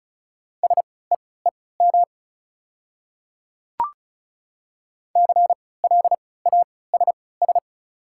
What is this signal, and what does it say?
Morse code, heard as a keyed tone, about 36 wpm, spaced as for 20 wpm, 700 Hz. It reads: SEEM CLASS
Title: 3x Character Spacing - Top 500 Words - 35wpm